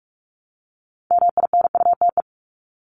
Morse code, heard as a keyed tone, about 44 wpm, 700 Hz. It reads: MINUTE
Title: Top 500 Words Repeated 45wpm